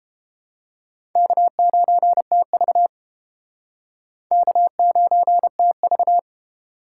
Morse code, heard as a keyed tone, about 30 wpm, 700 Hz. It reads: K9TV K9TV